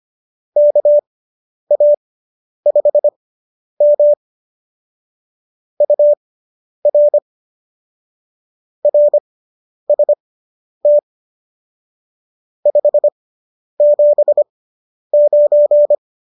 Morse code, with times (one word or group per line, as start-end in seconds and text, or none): 0.6-4.1 KA5M
5.8-7.2 UR
8.8-11.0 RST
12.7-15.9 579